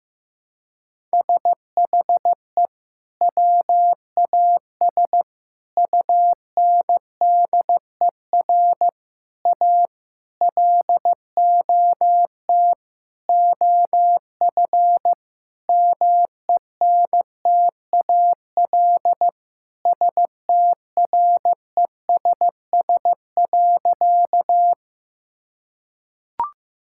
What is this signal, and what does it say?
Morse code, heard as a keyed tone, about 15 wpm, 700 Hz. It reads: SHE WAS UNDER A LOT OF MENTAL STRESS.
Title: Sentences from Top 1000 Words - Review Part 6 - 15wpm